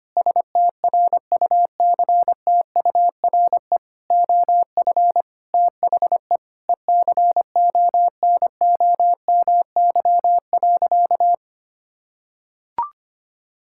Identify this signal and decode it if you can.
Morse code, heard as a keyed tone, about 25 wpm, 700 Hz. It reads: STRUCTURE OF THE ECONOMY.
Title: Sentences from Top 1000 Words - Review Part 6 - 25wpm